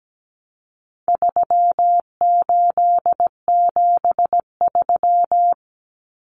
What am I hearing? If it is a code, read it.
Morse code, heard as a keyed tone, about 17 wpm, 700 Hz. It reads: 3873
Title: Numbers - 4-Digit 17wpm